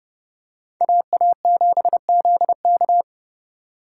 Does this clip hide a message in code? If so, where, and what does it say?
Yes: Morse code from 0.8 to 3.0 s, heard as a keyed tone, about 27 wpm, 700 Hz. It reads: AA7ZK